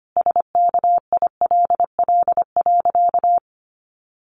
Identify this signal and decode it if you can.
Morse code, heard as a keyed tone, about 25 wpm, 700 Hz. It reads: SKILL.
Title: Sentences from Top 1200 Words 25wpm